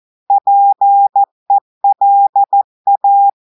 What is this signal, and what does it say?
Morse code, heard as a keyed tone, about 14 wpm, 800 Hz. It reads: PELA
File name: audios / portugais / portugais texte en clair 1 vitesse 14 mots minutes